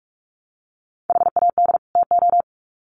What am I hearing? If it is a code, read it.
Morse code, heard as a keyed tone, about 44 wpm, 700 Hz. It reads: HAD TO
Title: Sets of 2 Words 45wpm